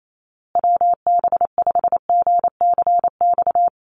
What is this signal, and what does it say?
Morse code, heard as a keyed tone, about 28 wpm, 700 Hz. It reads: WB5GCX
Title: USA Call Signs Repeated 2x 28wpm